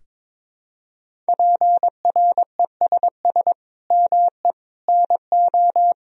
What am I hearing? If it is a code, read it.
Morse code, heard as a keyed tone, about 22 wpm, 700 Hz. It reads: PRESS ME NO
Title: Sentences from Top 1000 Words - Review Part 1 - 22wpm